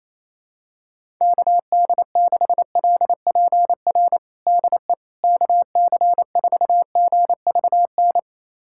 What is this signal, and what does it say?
Morse code, heard as a keyed tone, about 28 wpm, 700 Hz. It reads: KD6LPR DE KC4GVN
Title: USA Callsign DE Callsign - 6 Characters - 28wpm